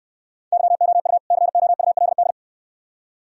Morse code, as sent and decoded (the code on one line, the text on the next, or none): -.-- --- ..- / -.-. --- ..- .-.. -..
YOU COULD